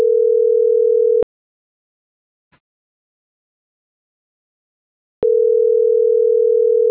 Ringback tone (North America), 440 + 480 Hz.
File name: CENTRAL SAGOT / out isai.hernandez